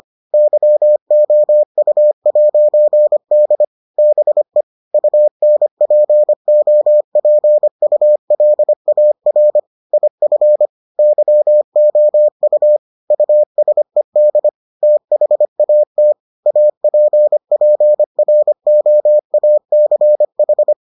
Morse code, heard as a keyed tone, about 25 wpm, 600 Hz. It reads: YOU'D BE UNPOPULAR IF YOU USED THAT APPROACH